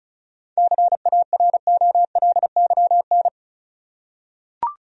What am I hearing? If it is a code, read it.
Morse code, heard as a keyed tone, about 36 wpm, 700 Hz. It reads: CAROLYN
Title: USA Common Names - 35wpm